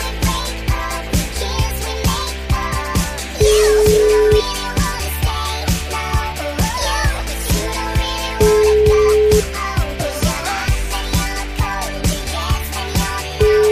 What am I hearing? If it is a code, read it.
Ringback tone (Europe, 425 Hz), with noise.